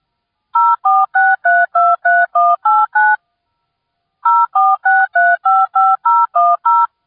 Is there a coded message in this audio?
Yes: DTMF from 0.5 to 6.9 s, heard as telephone touch tones, tones of about 220 ms. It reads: 046323189 046355010